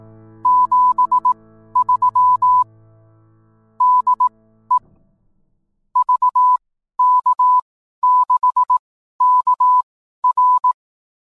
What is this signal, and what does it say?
Morse code, heard as a keyed tone, about 18 wpm, spaced as for 12 wpm, 1000 Hz. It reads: 73 DE VK6KR
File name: newswest